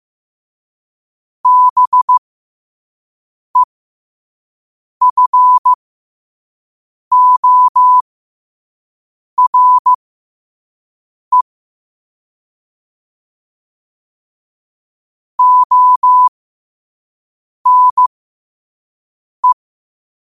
Morse code, heard as a keyed tone, about 15 wpm, spaced as for 5 wpm, 1000 Hz. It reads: BEFORE ONE
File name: qotdfiles / 5wpm